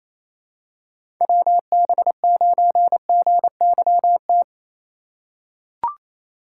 Morse code, heard as a keyed tone, about 27 wpm, 700 Hz. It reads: WB9GYT
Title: USA Call Signs - 4x Character Spacing - 28wpm